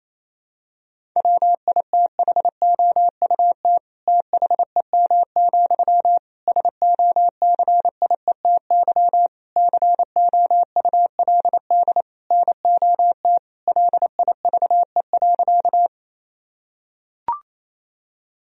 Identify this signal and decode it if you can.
Morse code, heard as a keyed tone, about 27 wpm, 700 Hz. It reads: WITHOUT THEM, SOCIETY COULD NOT LIVE.
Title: Sentences from Top 1000 Words - Review Part 4 - 28wpm